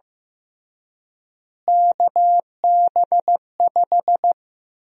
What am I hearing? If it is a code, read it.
Morse code, heard as a keyed tone, about 15 wpm, 700 Hz. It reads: KB5
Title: USA Call Sign Prefixes - Repeated 2x - 15wpm